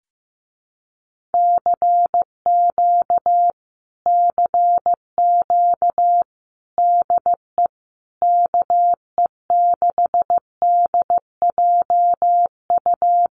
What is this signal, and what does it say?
Morse code, heard as a keyed tone, about 15 wpm, 700 Hz. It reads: CQ CQ DE KE6DJU
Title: Common QSO Phrases Encore 15wpm